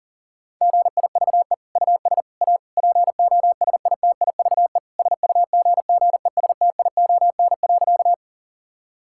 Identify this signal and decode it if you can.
Morse code, heard as a keyed tone, about 40 wpm, 700 Hz. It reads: GIVE US A POSITIVE SUGGESTION.
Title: Sentences from Top 1600 Words 40wpm